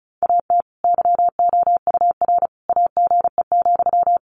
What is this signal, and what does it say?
Morse code, heard as a keyed tone, about 34 wpm, 700 Hz. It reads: AT YOUR AGE,
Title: Sentences from Top 1500 Words 35wpm